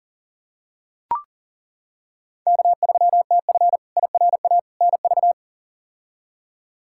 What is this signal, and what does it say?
Morse code, heard as a keyed tone, about 40 wpm, 700 Hz. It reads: K3TF IRA NV
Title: K1USN SST QSO Practice - 40wpm